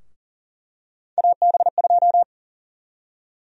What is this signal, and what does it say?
Morse code, heard as a keyed tone, about 40 wpm, 700 Hz. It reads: AB2
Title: USA Call Sign Prefixes - Repeated 2x - 40wpm